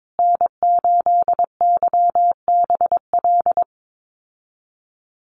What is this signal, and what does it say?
Morse code, heard as a keyed tone, about 22 wpm, 690 Hz. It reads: N8YBL
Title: USA Call Signs Repeated 3x 22wpm